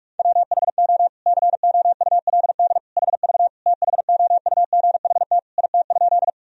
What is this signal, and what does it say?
Morse code, heard as a keyed tone, about 46 wpm, 700 Hz. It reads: WHO COULD HV THOUGHT IT?
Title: Sentences from Top 300 Words and Abbreviations 45wpm